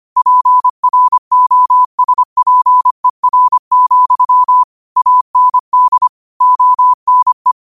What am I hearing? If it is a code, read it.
Morse code, heard as a keyed tone, about 25 wpm, 1000 Hz. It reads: PROSPER, AND ONE
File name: qotdfiles / 25wpm